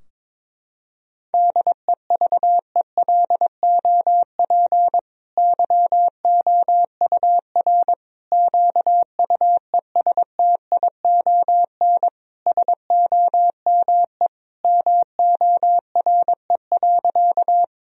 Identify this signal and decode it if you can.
Morse code, heard as a keyed tone, about 22 wpm, 700 Hz. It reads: DEVELOP YOUR QUESTION SOME MORE.
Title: Sentences from Top 1000 Words - Review Part 2 - 22wpm